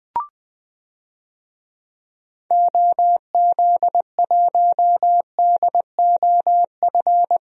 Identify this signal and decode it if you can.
Morse code, heard as a keyed tone, about 20 wpm, 700 Hz. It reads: OZ1DOF
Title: DXCC-100 Call Signs - 6 Characters - Repeated 2x - 20wpm